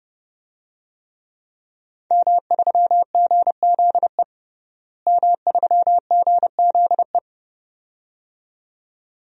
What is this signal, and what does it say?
Morse code, heard as a keyed tone, about 32 wpm, 700 Hz. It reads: M3GZE M3GZE
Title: DXCC-100 Call Signs - 5 Characters - Repeated 2x - 30wpm